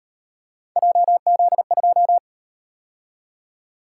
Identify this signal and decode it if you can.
Morse code, heard as a keyed tone, about 38 wpm, 700 Hz. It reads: JZ2